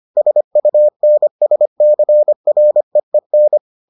Morse code, heard as a keyed tone, about 25 wpm, 600 Hz. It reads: SUNSCREEN